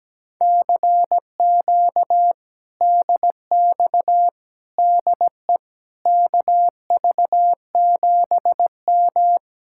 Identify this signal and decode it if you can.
Morse code, heard as a keyed tone, about 17 wpm, 700 Hz. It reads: CQ DX DE KV7M
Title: Speed-Racing Common QSO Phrases - 17wpm